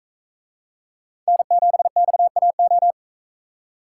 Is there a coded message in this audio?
Yes: Morse code from 1.3 to 2.9 s, heard as a keyed tone, about 41 wpm, 700 Hz. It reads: N7XAO